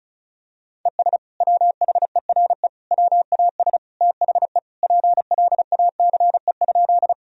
Morse code, heard as a keyed tone, about 34 wpm, 700 Hz. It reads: ES WHERE WAS THE PLACE?
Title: Sentences from Top 200 Words and Abbreviations 35wpm